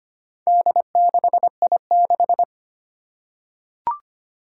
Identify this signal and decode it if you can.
Morse code, heard as a keyed tone, about 25 wpm, 700 Hz. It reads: D6I6
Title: Four Challenging Letter-Number Combinations 25wpm